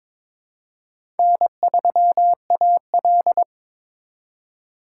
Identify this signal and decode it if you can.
Morse code, heard as a keyed tone, about 22 wpm, 700 Hz. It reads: N3AL
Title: USA Call Signs - 3x Character Spacing - 22wpm